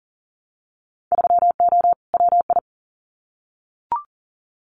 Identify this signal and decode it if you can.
Morse code, heard as a keyed tone, about 40 wpm, 700 Hz. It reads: 3O WI
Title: Winter Field Day Exchanges 40wpm